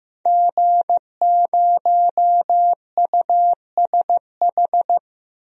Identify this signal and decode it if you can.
Morse code, heard as a keyed tone, about 15 wpm, 700 Hz. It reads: G0USH
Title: DXCC-100 Call Signs - 5 Characters - Repeated 2x - 15wpm